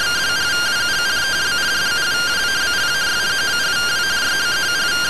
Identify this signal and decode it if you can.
Two-tone teleprinter signal, about 170 Hz shift.